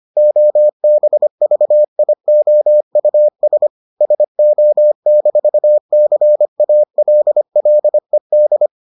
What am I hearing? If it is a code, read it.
Morse code, heard as a keyed tone, about 25 wpm, 600 Hz. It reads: OBVIOUS SO-CALLED